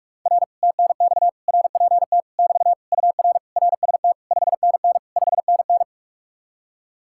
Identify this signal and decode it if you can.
Morse code, heard as a keyed tone, about 49 wpm, 700 Hz. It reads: R TNX RPT <BT> UR RST 5NN 5NN